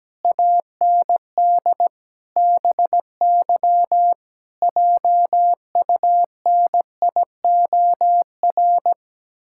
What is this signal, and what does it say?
Morse code, heard as a keyed tone, about 17 wpm, 700 Hz. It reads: AND BY JUNIOR
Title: Pangram Challenge v2 - 17wpm